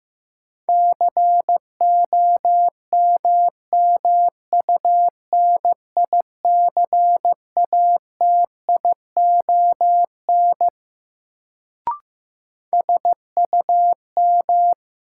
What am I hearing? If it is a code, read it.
Morse code, heard as a keyed tone, about 15 wpm, 700 Hz. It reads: COMMUNICATION SUM
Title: Sentences from Top 1000 Words - Review Part 6 - New Word Drill - 15wpm